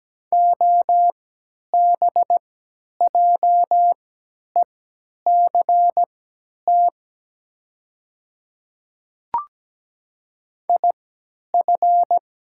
Morse code, heard as a keyed tone, about 17 wpm, spaced as for 10 wpm, 700 Hz. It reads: OBJECT IF
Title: 3x Character Spacing - Top 500 Words - 17wpm